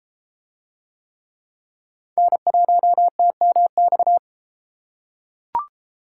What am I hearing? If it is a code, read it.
Morse code, heard as a keyed tone, about 31 wpm, 700 Hz. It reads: N1TMX